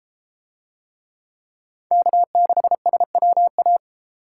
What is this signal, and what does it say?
Morse code, heard as a keyed tone, about 32 wpm, 700 Hz. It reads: K6SWA